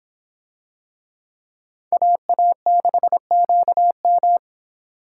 Morse code, heard as a keyed tone, about 26 wpm, 700 Hz. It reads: AA6QM